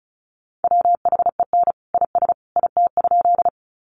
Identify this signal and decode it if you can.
Morse code, heard as a keyed tone, about 34 wpm, 700 Hz. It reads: WHEN IS IT?
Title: Sentences from Top 100 Words 35wpm